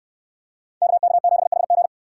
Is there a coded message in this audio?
Yes: Morse code from 0.8 to 1.9 s, heard as a keyed tone, about 68 wpm, 700 Hz. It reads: KK7HG